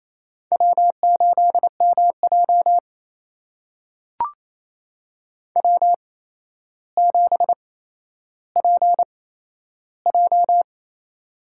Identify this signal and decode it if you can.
Morse code, heard as a keyed tone, about 28 wpm, spaced as for 12 wpm, 700 Hz. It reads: W8MJ W7PJ
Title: USA Call Signs - 4 Characters - 8x Character Spacing - 28wpm